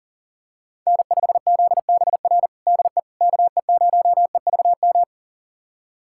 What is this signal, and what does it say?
Morse code, heard as a keyed tone, about 37 wpm, 700 Hz. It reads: N5ZBR DE KE0EVM
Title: USA Callsign DE Callsign 40wpm